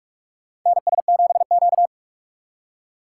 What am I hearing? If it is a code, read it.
Morse code, heard as a keyed tone, about 43 wpm, 700 Hz. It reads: NS7Q
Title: USA Call Signs - 4 Characters - 8x Character Spacing - 45wpm